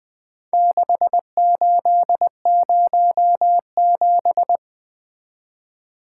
Morse code, heard as a keyed tone, about 20 wpm, 700 Hz. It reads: 6807
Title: Four Letter-Number Combinations - 20wpm